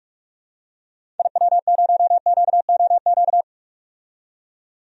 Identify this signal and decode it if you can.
Morse code, heard as a keyed tone, about 44 wpm, 700 Hz. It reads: IW0QOQ
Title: DXCC-100 Call Signs - 6 Characters - 45wpm